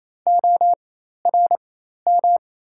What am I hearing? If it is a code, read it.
Morse code, heard as a keyed tone, about 28 wpm, spaced as for 13 wpm, 700 Hz. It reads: ORM
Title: 4x Character Spacing - Top 300 Words - 28wpm